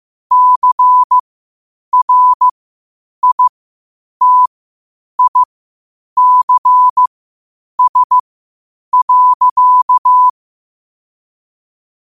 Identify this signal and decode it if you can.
Morse code, heard as a keyed tone, about 15 wpm, spaced as for 8 wpm, 1000 Hz. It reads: CRITICS.
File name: qotdfiles / 8wpm